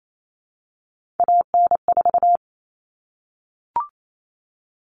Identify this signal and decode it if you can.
Morse code, heard as a keyed tone, about 28 wpm, 700 Hz. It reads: AN4
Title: All Cut Numbers - 3-Digit 28wpm